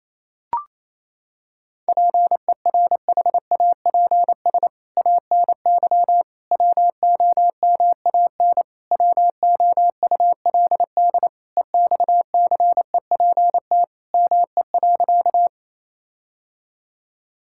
Morse code, heard as a keyed tone, about 28 wpm, 700 Hz. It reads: PERHAPS ANY WOMAN WOULD EXCEPT ME.
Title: Sentences from Top 1600 Words 28wpm